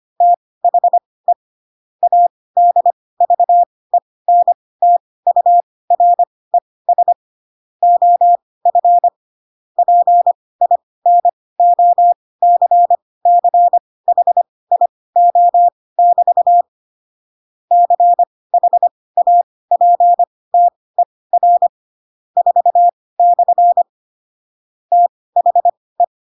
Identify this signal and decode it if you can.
Morse code, heard as a keyed tone, about 25 wpm, spaced as for 18 wpm, 700 Hz. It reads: THE ADVENTURES OF PINOCCHIO<BT> CHAPTER 4/ THE
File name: Pinocchio / Pinocchio ch4 12 25 ik1hge